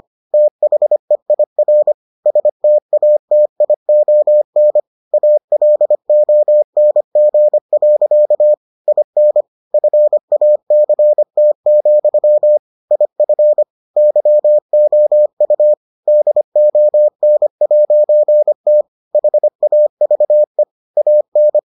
Morse code, heard as a keyed tone, about 25 wpm, 600 Hz. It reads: THEIR STATION ALONG. IN FACT, IF YOU DON'T HAVE AN